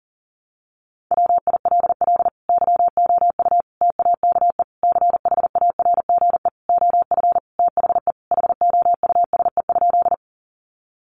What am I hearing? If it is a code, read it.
Morse code, heard as a keyed tone, about 40 wpm, 700 Hz. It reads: WILL YOU TAKE CHARGE OF THE HOUSE?
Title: Sentences from Top 900 Words 40wpm